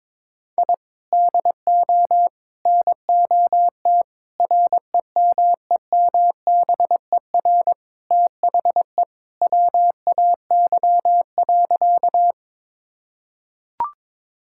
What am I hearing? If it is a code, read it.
Morse code, heard as a keyed tone, about 21 wpm, 700 Hz. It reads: I DO NOT REMEMBER THE WAY.
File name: practice / 22wpm / Sentences with 4-Word Phrases - 22wpm